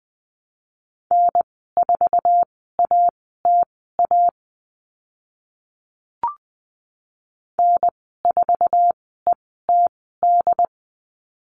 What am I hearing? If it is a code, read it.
Morse code, heard as a keyed tone, about 20 wpm, spaced as for 14 wpm, 700 Hz. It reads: N4ATA N4ETD
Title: 088 Lesson 15g - Number 5 - Call Signs 20wpm